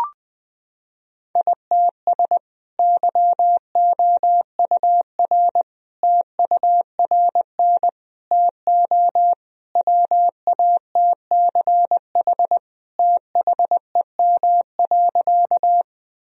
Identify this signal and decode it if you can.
Morse code, heard as a keyed tone, about 21 wpm, 700 Hz. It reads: ITS YOUR TURN TO WATCH THEM.
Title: Sentences from Top 1000 Words - Review Part 1 - 20wpm